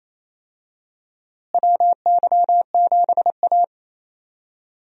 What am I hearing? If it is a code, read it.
Morse code, heard as a keyed tone, about 28 wpm, 700 Hz. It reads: WY7A
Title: USA Call Signs - 4 Characters - Repeated 3x - 28wpm